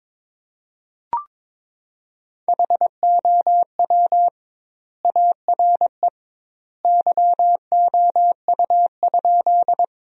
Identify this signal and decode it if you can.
Morse code, heard as a keyed tone, about 22 wpm, 700 Hz. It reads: HOW ARE YOU?
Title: 2x Word-Spacing - Sentences from Top 100 Words - 22wpm